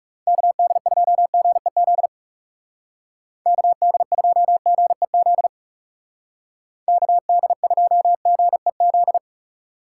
Morse code, heard as a keyed tone, about 36 wpm, 700 Hz. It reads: KD2GEZ KD2GEZ KD2GEZ
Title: Speed Racing - USA 2x3 Call Signs 30wpm